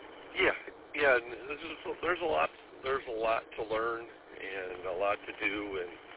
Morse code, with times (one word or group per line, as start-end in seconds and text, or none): none